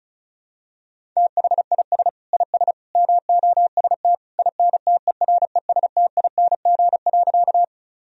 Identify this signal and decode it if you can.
Morse code, heard as a keyed tone, about 34 wpm, 700 Hz. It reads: THIS IS MOST INTERESTING.
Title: Sentences from Top 1000 Words - Review Part 1 - 35wpm